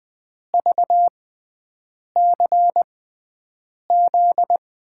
Morse code, heard as a keyed tone, about 20 wpm, spaced as for 7 wpm, 700 Hz. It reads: VCZ
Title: USA Call Signs - 6 Characters - 6x Character Spacing - 20wpm